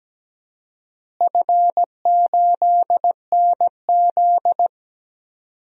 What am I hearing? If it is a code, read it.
Morse code, heard as a keyed tone, about 17 wpm, 700 Hz. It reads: F8NZ
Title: DXCC-100 Call Signs - 4 Characters - Repeated 2x - 17wpm